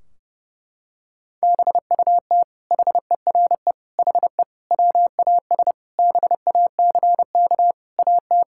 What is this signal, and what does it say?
Morse code, heard as a keyed tone, about 30 wpm, 700 Hz. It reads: BUT HERE HE WAS BACK AT
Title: Sentences from Top 1600 Words 30wpm